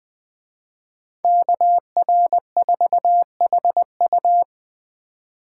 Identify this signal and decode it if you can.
Morse code, heard as a keyed tone, about 20 wpm, 700 Hz. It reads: KR4HU
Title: Speed Racing - USA 2x2 Call Signs 17wpm